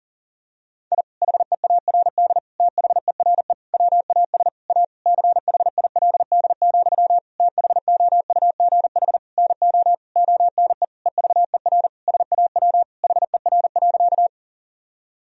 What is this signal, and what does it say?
Morse code, heard as a keyed tone, about 40 wpm, 700 Hz. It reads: I HEARD THERE WAS A CHILD, THOUGH NO ONE EVER SAW HER.
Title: Sentences from Top 1000 Words - Review Part 2 - 40wpm